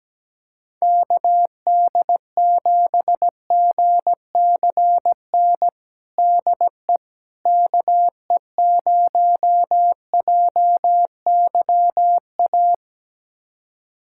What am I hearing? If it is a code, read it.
Morse code, heard as a keyed tone, about 17 wpm, 700 Hz. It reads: KD7GCN DE KE0JYA
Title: USA Callsign DE Callsign 17wpm